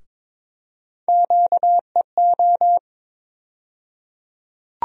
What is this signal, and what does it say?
Morse code, heard as a keyed tone, about 22 wpm, 700 Hz. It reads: QEO
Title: USA Call Sign Postfixes - 3 Letters 22wpm